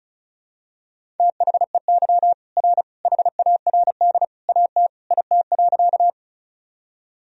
Morse code, heard as a keyed tone, about 34 wpm, 700 Hz. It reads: THEY R HARD AT IT.
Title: Sentences from Top 300 Words and Abbreviations 35wpm